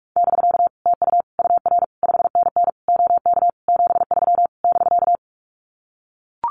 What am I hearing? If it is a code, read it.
Morse code, heard as a keyed tone, about 44 wpm, 700 Hz. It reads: <BK> TU UR 5NN OK 73 <BK>
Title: Common QSO Phrases Encore 45wpm